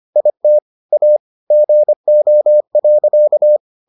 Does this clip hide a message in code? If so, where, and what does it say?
Yes: Morse code from 0.2 to 3.6 s, heard as a keyed tone, about 25 wpm, 600 Hz. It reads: IT A GO.